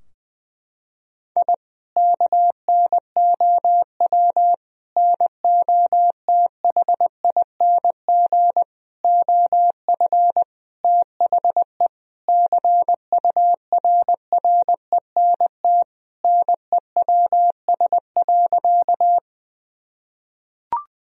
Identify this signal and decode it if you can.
Morse code, heard as a keyed tone, about 21 wpm, 700 Hz. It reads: I KNOW NOTHING OF THE CURRENT NEWS.
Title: Sentences from Top 1000 Words - Review Part 4 - 20wpm